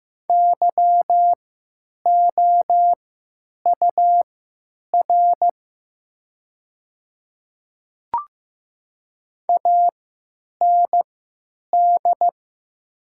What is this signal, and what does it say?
Morse code, heard as a keyed tone, about 15 wpm, spaced as for 9 wpm, 690 Hz. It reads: YOUR AND
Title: 3x Character Spacing - Top 100 Words - 15wpm